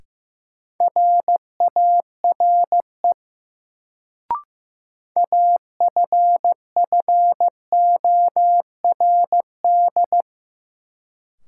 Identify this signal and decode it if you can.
Morse code, heard as a keyed tone, about 15 wpm, 700 Hz. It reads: RARE AFFORD
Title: Sentences from Top 1100 Words - New Word Drill Alt - 15wpm